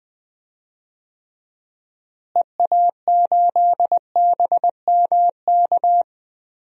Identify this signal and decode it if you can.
Morse code, heard as a keyed tone, about 20 wpm, 700 Hz. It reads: EA8BMK
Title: DXCC-100 Call Signs - 6 Characters - Repeated 2x - 20wpm